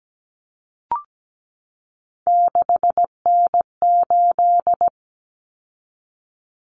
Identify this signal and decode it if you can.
Morse code, heard as a keyed tone, about 17 wpm, 700 Hz. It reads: 6N8
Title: Common Cut Numbers - 3-Digit 17wpm